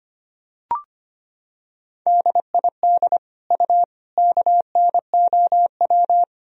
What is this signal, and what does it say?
Morse code, heard as a keyed tone, about 25 wpm, 700 Hz. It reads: DID U KNOW
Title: Sentences from Top 100 Words and Abbreviations 25wpm